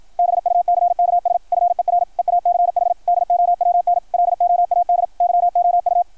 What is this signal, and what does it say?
Morse code, heard as a keyed tone, about 53 wpm, 690 Hz, with radio noise. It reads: BUCKS FER EACH DOWNLOAD YOU
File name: archived / CRSnet